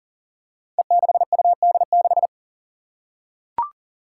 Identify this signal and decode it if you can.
Morse code, heard as a keyed tone, about 38 wpm, 700 Hz. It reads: E6UD6
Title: Five Challenging Letter-Number Combinations 40wpm